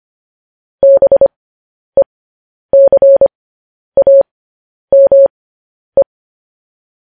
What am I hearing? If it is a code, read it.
Morse code, heard as a keyed tone, about 25 wpm, spaced as for 10 wpm, 550 Hz. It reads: BECAME
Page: news headlines in morse code at 10 wpm